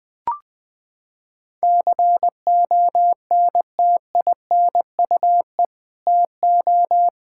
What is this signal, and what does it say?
Morse code, heard as a keyed tone, about 20 wpm, 700 Hz. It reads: CONTINUE TO
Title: Sentences from Top 1000 Words - Review Part 6 - 20wpm